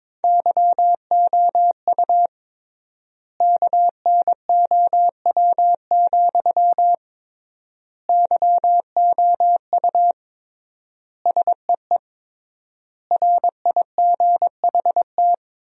Morse code, heard as a keyed tone, about 22 wpm, 700 Hz. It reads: YOU KNOW, YOU SEE RIGHT
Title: 3x Word-Spacing - Sentences from Top 200 Words - 22wpm